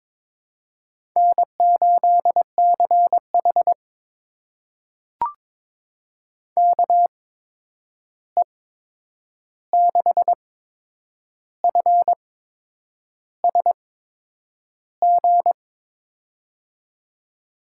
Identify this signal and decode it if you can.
Morse code, heard as a keyed tone, about 22 wpm, spaced as for 8 wpm, 700 Hz. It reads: N8CH KE6FSG
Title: USA Call Signs - 8x Character Spacing - 22wpm